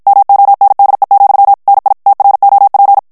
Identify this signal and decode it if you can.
Morse code, heard as a keyed tone, about 56 wpm, 800 Hz. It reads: MONDE, NI TROP
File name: FRANCAIS TEXTE CLAIR 2 VITESSE 50 MOTS MINUTES